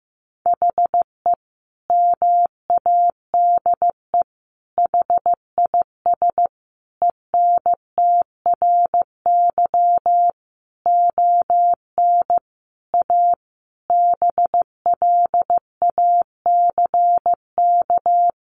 Morse code, heard as a keyed tone, about 15 wpm, 700 Hz. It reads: HE MADE HIS ENTRY ON A BLACK
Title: Sentences from Top 1500 Words 15wpm